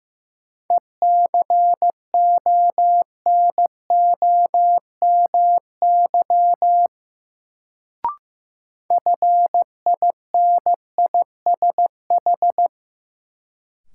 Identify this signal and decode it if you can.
Morse code, heard as a keyed tone, about 15 wpm, 700 Hz. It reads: ECONOMY FINISH
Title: Sentences from Top 1200 Words - New Word Drill Alt - 15wpm